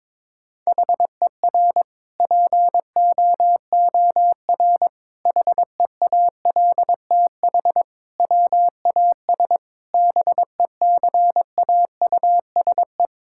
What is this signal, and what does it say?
Morse code, heard as a keyed tone, about 22 wpm, 700 Hz. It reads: HER POOR HEALTH WAS BECAUSE
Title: Sentences from Top 1000 Words - Review Part 6 - 22wpm